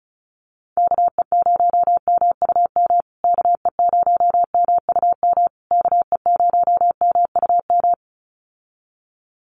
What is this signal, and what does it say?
Morse code, heard as a keyed tone, about 34 wpm, 700 Hz. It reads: KE0MUM KE0MUM KE0MUM
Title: USA Call Signs Repeated 3x 35wpm